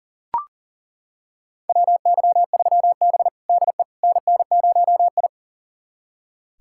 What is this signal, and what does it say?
Morse code, heard as a keyed tone, about 40 wpm, 700 Hz. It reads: WY3B DE NN0I
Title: USA Callsign DE Callsign - 4 Characters - 40wpm